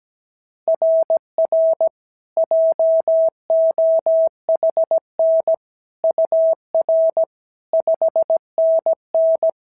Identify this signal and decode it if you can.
Morse code, heard as a keyed tone, about 17 wpm, 690 Hz. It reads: RR JOHN UR 5NN
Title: SOTA QSO Practice 17wpm